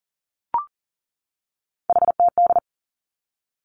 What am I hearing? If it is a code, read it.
Morse code, heard as a keyed tone, about 42 wpm, 690 Hz. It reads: HTD
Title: Three Challenging Letter-Number Combinations 40wpm